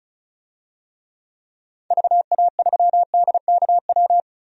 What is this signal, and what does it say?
Morse code, heard as a keyed tone, about 36 wpm, 700 Hz. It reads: VA3DKW
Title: DXCC-100 Call Signs - 6 Characters - Repeated 3x - 35wpm